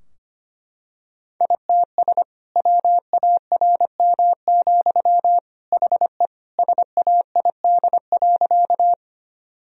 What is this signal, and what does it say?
Morse code, heard as a keyed tone, about 25 wpm, 700 Hz. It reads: ITS WARM, HE SAID.
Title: Sentences from Top 1000 Words - Review Part 2 - 25wpm